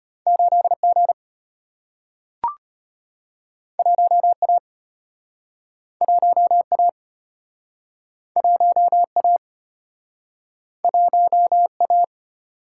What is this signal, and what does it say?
Morse code, heard as a keyed tone, about 34 wpm, 700 Hz. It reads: 8G 1A 1A 1A 1A